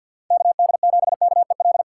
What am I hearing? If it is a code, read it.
Morse code, heard as a keyed tone, about 50 wpm, 700 Hz. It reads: KD7KEL
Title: USA Call Signs - 6x Character Spacing - 50wpm